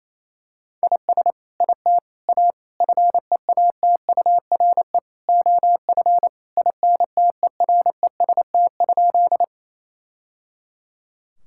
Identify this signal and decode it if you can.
Morse code, heard as a keyed tone, about 28 wpm, 700 Hz. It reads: IS IT A FEATURE OF INTEREST?
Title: Sentences from Top 1000 Words - Review Part 5 - 28wpm